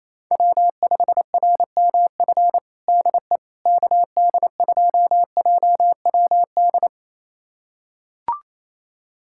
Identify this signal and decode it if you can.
Morse code, heard as a keyed tone, about 27 wpm, 700 Hz. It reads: W5RMF DE KD2JWD
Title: USA Callsign DE Callsign 28wpm